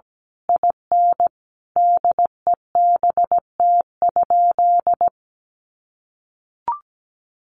Morse code, heard as a keyed tone, about 17 wpm, 700 Hz. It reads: IN DEBT?
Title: Sentences from Top 1300 Words 17wpm